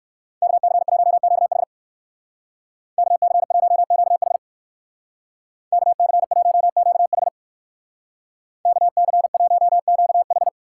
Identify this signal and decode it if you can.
Morse code, heard as a keyed tone, about 63 wpm, 700 Hz. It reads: KC1QH KC1QH KC1QH KC1QH